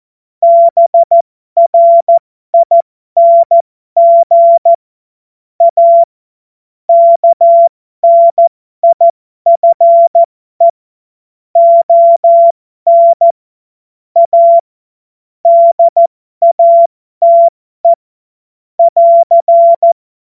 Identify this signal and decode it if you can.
Morse code, heard as a keyed tone, about 14 wpm, spaced as for 12 wpm, 680 Hz. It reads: BRING A KNIFE ON A DATE <AR>
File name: Morsekurs / Witze